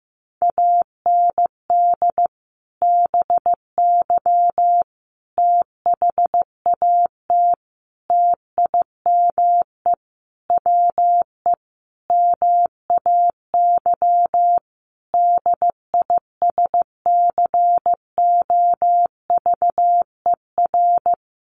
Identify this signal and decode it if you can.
Morse code, heard as a keyed tone, about 15 wpm, 700 Hz. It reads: AND BY THAT TIME WE MAY DISCOVER
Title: Sentences from Top 1100 Words 15wpm